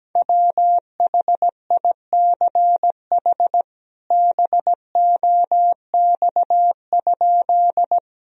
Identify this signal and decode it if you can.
Morse code, heard as a keyed tone, about 17 wpm, 700 Hz. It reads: WHICH BOX?
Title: Sentences from Top 1000 Words - Review Part 2 - 17wpm